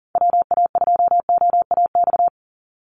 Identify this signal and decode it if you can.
Morse code, heard as a keyed tone, about 40 wpm, 700 Hz. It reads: WA2OAX
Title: Call Signs Repeated 2x - 40wpm